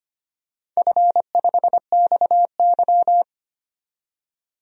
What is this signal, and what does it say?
Morse code, heard as a keyed tone, about 25 wpm, 700 Hz. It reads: F5XY
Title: DXCC-100 Call Signs - 4 Characters - 25wpm